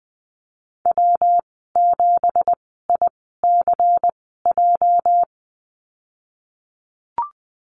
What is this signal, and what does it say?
Morse code, heard as a keyed tone, about 20 wpm, spaced as for 15 wpm, 700 Hz. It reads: W7ICJ
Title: 213 Lesson 34g - Number 8 - Call Signs 20wpm